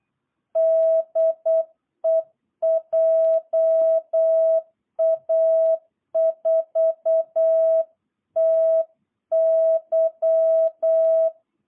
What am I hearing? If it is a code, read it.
Morse code, heard as a keyed tone, about 8 wpm, 660 Hz. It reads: DE JA4TY
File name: jh8cbh / 2022cw